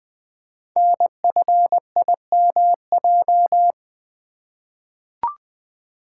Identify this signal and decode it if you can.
Morse code, heard as a keyed tone, about 20 wpm, 700 Hz. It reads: NFIMJ
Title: Code Groups - 5 Letters - 20wpm